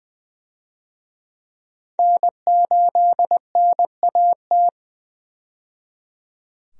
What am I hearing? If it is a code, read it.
Morse code, heard as a keyed tone, about 20 wpm, 700 Hz. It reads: N8NAT
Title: USA Call Signs - 5 Characters - 20wpm